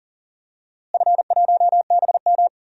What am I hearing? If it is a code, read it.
Morse code, heard as a keyed tone, about 40 wpm, 700 Hz. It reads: F1BM
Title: DXCC-100 Call Signs - 4 Characters - Repeated 3x - 40wpm